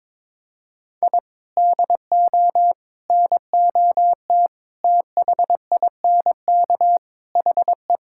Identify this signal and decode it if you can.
Morse code, heard as a keyed tone, about 22 wpm, 700 Hz. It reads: I DO NOT THINK HE
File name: practice / 22wpm / Sentences with 4-Word Phrases - 22wpm